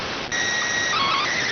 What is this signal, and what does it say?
SSTV (slow-scan television), PD 290.